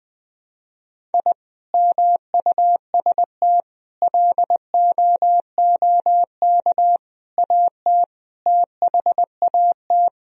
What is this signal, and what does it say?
Morse code, heard as a keyed tone, about 20 wpm, 700 Hz. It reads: I MUST LOOK AT THAT
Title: Sentences from Top 1000 Words - Review Part 2 - 20wpm